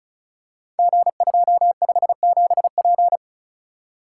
Morse code, heard as a keyed tone, about 36 wpm, 690 Hz. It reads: G257P
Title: Code Groups - 5 Letter:Numbers - 35wpm